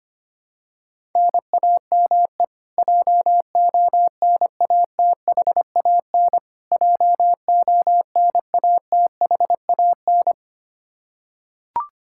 Morse code, heard as a keyed tone, about 25 wpm, 700 Hz. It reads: NAME JONATHAN JONATHAN
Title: Practice QSO - Exchange 1 - Beginning - 25wpm